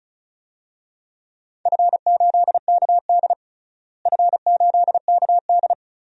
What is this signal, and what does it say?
Morse code, heard as a keyed tone, about 34 wpm, 700 Hz. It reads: F8KD F8KD
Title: DXCC-100 Call Signs - 4 Characters - Repeated 2x - 35wpm